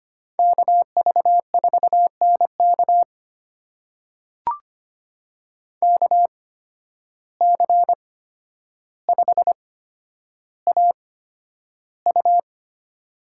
Morse code, heard as a keyed tone, about 25 wpm, spaced as for 11 wpm, 700 Hz. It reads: KV4NK KC5AU
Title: USA Call Signs - 5 Characters - 8x Character Spacing - 25wpm